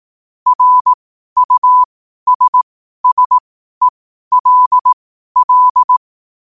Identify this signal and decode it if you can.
Morse code, heard as a keyed tone, about 18 wpm, spaced as for 12 wpm, 1000 Hz. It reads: RUSSELL
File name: qotdfiles / 12wpm